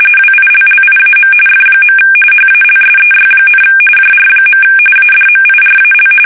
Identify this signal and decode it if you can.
RTTY, about 45.45 baud. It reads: WILLIAM F. HALSEY, JR. ADMIRAL
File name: fsk850 / battleshipnj